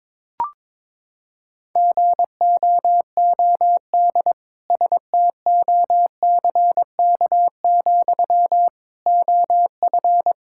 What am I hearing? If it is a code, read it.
Morse code, heard as a keyed tone, about 22 wpm, 700 Hz. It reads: GOOD STOCK, OF
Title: Sentences from Top 1000 Words - Review Part 5 - 22wpm